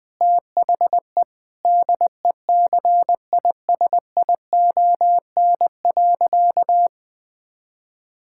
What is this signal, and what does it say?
Morse code, heard as a keyed tone, about 20 wpm, 700 Hz. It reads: THE DECISION.
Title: Sentences with 6-Word Phrases - 20wpm